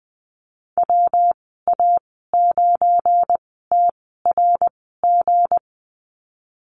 Morse code, heard as a keyed tone, about 20 wpm, spaced as for 15 wpm, 700 Hz. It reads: WA9TRG
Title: 186 Lesson 30g - Letter b - Call Signs 20wpm